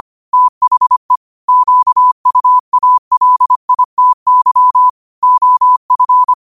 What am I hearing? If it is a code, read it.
Morse code, heard as a keyed tone, about 25 wpm, 1000 Hz. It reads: THE QUALITY OF
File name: qotdfiles / 25wpm